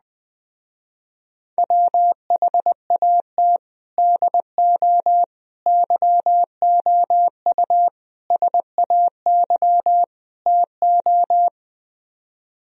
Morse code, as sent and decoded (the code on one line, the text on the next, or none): .-- .... .- - / -.. --- / -.-- --- ..- / ... .- -.-- / - ---
WHAT DO YOU SAY TO